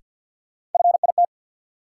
Morse code, heard as a keyed tone, about 51 wpm, 700 Hz. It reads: FIT
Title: Sentences from Top 800 Words - New Word Drill - 50wpm